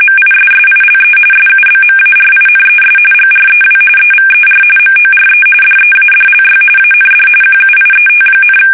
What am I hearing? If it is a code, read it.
RTTY, about 45.45 baud. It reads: MISSILE SUBMARINES GREW EVER MORE DEADLY AND QUIET,
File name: fsk850 / usnavy5